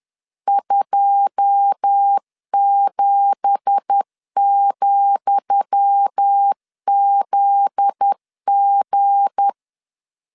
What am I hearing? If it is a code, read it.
Morse code, heard as a keyed tone, about 10 wpm, 800 Hz. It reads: 27,ZG
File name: LEZIONI / 10 SIGNS / 10D COMA 10wpm 3013